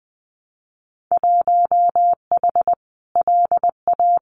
Morse code, heard as a keyed tone, about 20 wpm, 700 Hz. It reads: 1H LA
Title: Winter Field Day Exchanges 20wpm